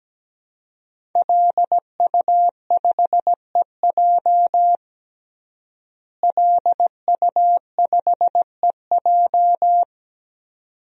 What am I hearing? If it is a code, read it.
Morse code, heard as a keyed tone, about 17 wpm, 700 Hz. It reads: LU5EJ LU5EJ
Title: DXCC-100 Call Signs - 5 Characters - Repeated 2x - 17wpm